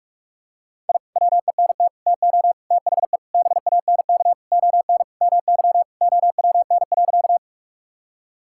Morse code, heard as a keyed tone, about 46 wpm, 700 Hz. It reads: I WENT TO THE BANK ON MY OWN.
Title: Sentences from Top 1000 Words - Review Part 6 - 45wpm